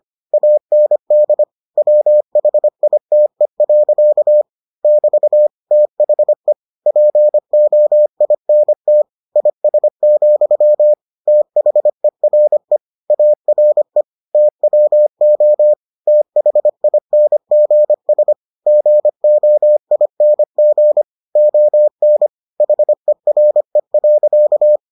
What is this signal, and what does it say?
Morse code, heard as a keyed tone, about 25 wpm, 600 Hz. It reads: AND WHITE. <BT> THE POINT IS, THERE ARE TWO THINGS GOING ON HERE.